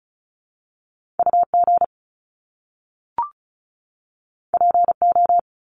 Morse code, heard as a keyed tone, about 36 wpm, 700 Hz. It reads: UG PO
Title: Top Two Letter Combinations in Words 35wpm